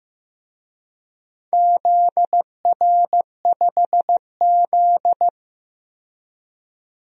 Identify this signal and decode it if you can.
Morse code, heard as a keyed tone, about 15 wpm, 700 Hz. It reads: ZR5Z
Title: DXCC-100 Call Signs - 4 Characters - 15wpm